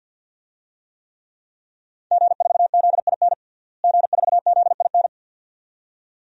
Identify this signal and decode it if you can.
Morse code, heard as a keyed tone, about 52 wpm, 700 Hz. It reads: G4ZIN G4ZIN
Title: DXCC-100 Call Signs - 5 Characters - Repeated 2x - 50wpm